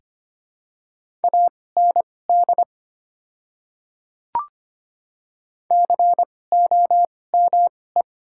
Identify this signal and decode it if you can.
Morse code, heard as a keyed tone, about 25 wpm, spaced as for 18 wpm, 700 Hz. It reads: AND COME
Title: 2x Character Spacing - Top 500 Words - 25wpm